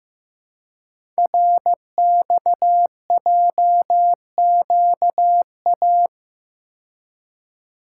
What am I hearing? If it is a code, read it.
Morse code, heard as a keyed tone, about 15 wpm, 700 Hz. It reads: RXJQA
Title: Code Groups - 5 Letters - 15wpm